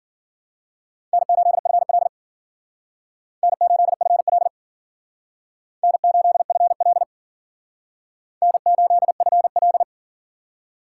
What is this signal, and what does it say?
Morse code, heard as a keyed tone, about 53 wpm, 690 Hz. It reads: N8FL N8FL N8FL N8FL